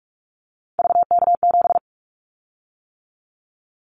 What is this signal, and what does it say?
Morse code, heard as a keyed tone, about 44 wpm, 700 Hz. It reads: VK7
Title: Callsign Prefixes for Top 50 DXCC - 45wpm